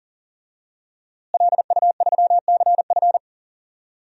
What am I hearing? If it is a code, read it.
Morse code, heard as a keyed tone, about 40 wpm, 700 Hz. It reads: LU3CF
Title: DXCC-100 Call Signs - 5 Characters - Repeated 3x - 40wpm